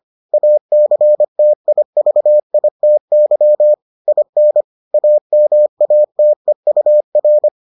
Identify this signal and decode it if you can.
Morse code, heard as a keyed tone, about 25 wpm, 600 Hz. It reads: ACTIVITY IN AMATEUR